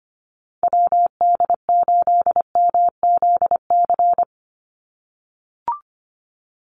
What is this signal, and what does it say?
Morse code, heard as a keyed tone, about 25 wpm, 700 Hz. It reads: WD8MZC
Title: USA Call Signs 25wpm